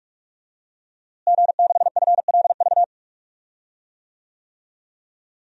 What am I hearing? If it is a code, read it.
Morse code, heard as a keyed tone, about 43 wpm, 700 Hz. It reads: G6FLV
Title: DXCC-100 Call Signs - 5 Characters - Repeated 3x - 45wpm